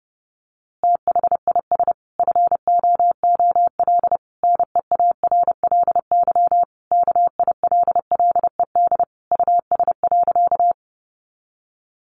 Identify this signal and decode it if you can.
Morse code, heard as a keyed tone, about 30 wpm, 700 Hz. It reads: THIS FOOL NEARLY KILLED US.
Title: A Sound Of Thunder 30wpm